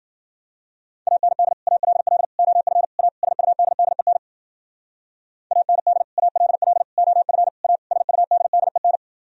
Morse code, heard as a keyed tone, about 51 wpm, 700 Hz. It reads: AND ALL OF A SUDDEN AND ALL OF A SUDDEN